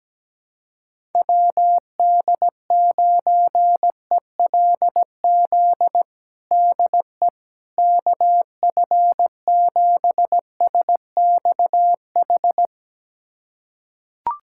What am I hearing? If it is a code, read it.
Morse code, heard as a keyed tone, about 17 wpm, 700 Hz. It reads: WD9ELZ DE KF7SXH
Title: USA Callsign DE Callsign - 6 Characters - 17wpm